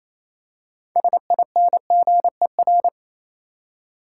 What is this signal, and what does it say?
Morse code, heard as a keyed tone, about 27 wpm, 700 Hz. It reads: SINGER
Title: Six Letter Words - 28wpm